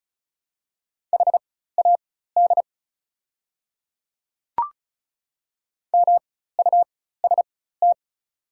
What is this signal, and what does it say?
Morse code, heard as a keyed tone, about 34 wpm, spaced as for 16 wpm, 690 Hz. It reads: HAD MUST
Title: 4x Character Spacing - Top 500 Words - 35wpm